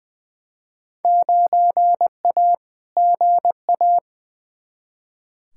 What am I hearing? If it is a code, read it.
Morse code, heard as a keyed tone, about 21 wpm, 700 Hz. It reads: 9A GA
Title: Field Day Exchanges Encore - 20wpm